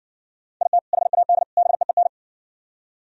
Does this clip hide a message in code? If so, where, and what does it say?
Yes: Morse code from 0.6 to 2.1 s, heard as a keyed tone, about 60 wpm, 700 Hz. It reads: IT HAD BEEN